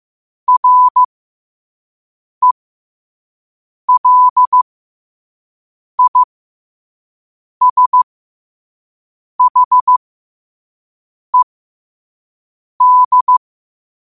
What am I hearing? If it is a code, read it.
Morse code, heard as a keyed tone, about 15 wpm, spaced as for 5 wpm, 1000 Hz. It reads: RELISHED
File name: qotdfiles / 5wpm